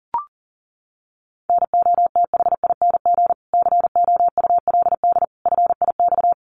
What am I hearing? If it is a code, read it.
Morse code, heard as a keyed tone, about 37 wpm, 700 Hz. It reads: NOTHING COULD FIX
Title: Sentences from Top 1400 Words 40wpm